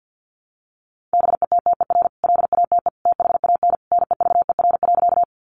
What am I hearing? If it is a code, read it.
Morse code, heard as a keyed tone, about 50 wpm, 700 Hz. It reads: BETTER LATE THAN NEVER.
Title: Sentences from Top 400 Words 50wpm